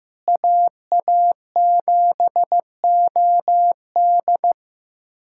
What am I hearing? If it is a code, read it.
Morse code, heard as a keyed tone, about 15 wpm, 700 Hz. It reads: AA7OD
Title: USA Callsign DE Callsign - 5 Characters - 15wpm